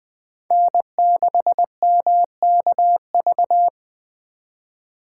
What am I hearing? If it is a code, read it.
Morse code, heard as a keyed tone, about 21 wpm, 700 Hz. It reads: N6MKV